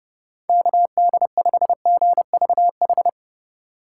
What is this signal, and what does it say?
Morse code, heard as a keyed tone, about 31 wpm, 700 Hz. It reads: KD5GVH